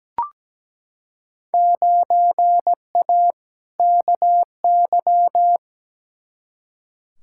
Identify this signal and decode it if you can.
Morse code, heard as a keyed tone, about 17 wpm, 700 Hz. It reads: 9A KY
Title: Field Day Exchanges Encore - 17wpm